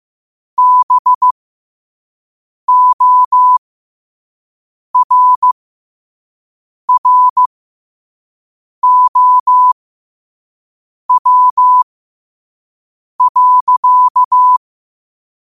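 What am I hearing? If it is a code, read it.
Morse code, heard as a keyed tone, about 15 wpm, spaced as for 5 wpm, 1000 Hz. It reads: BORROW.
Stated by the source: Morse quote of the day at 5 wpm.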